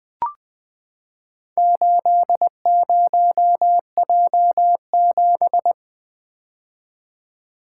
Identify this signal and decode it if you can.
Morse code, heard as a keyed tone, about 20 wpm, 700 Hz. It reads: 80J7
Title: Four Letter-Number Combinations - 20wpm